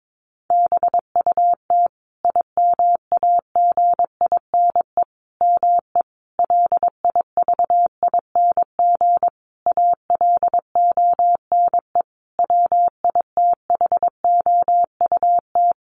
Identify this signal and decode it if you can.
Morse code, heard as a keyed tone, about 22 wpm, 700 Hz. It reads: BUT IMAGINE ME LIVING ALONE WITHOUT
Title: Sentences from Top 1400 Words 22wpm